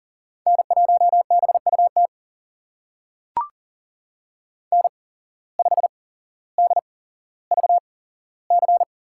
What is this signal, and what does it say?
Morse code, heard as a keyed tone, about 40 wpm, spaced as for 17 wpm, 700 Hz. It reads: N1BUT N5DVC
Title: USA Call Signs - 5 Characters - 8x Character Spacing - 40wpm